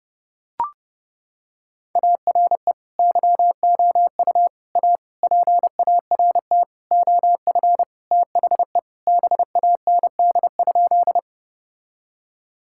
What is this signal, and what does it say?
Morse code, heard as a keyed tone, about 32 wpm, 700 Hz. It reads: ARE YOU A PART OF THE BAND?
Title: Sentences from Top 1600 Words 30wpm